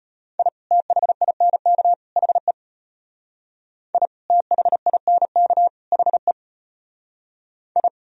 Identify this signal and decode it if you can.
Morse code, heard as a keyed tone, about 32 wpm, 700 Hz. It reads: I THINK HE I THINK HE I